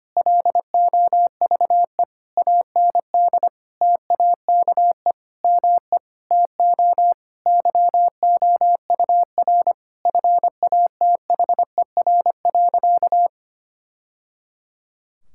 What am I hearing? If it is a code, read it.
Morse code, heard as a keyed tone, about 25 wpm, 700 Hz. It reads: LOVE AND TAKE ME TO YOUR FATHER.
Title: Sentences from Top 1000 Words - Review Part 6 - 25wpm